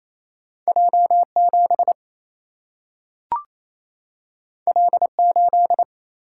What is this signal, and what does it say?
Morse code, heard as a keyed tone, about 28 wpm, 700 Hz. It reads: J 7 L 8
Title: Single-Letter Single-Number 28wpm